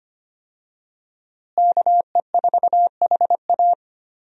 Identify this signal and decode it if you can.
Morse code, heard as a keyed tone, about 25 wpm, 700 Hz. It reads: KE4HA
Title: USA Call Signs - 5 Characters - Repeated 3x - 25wpm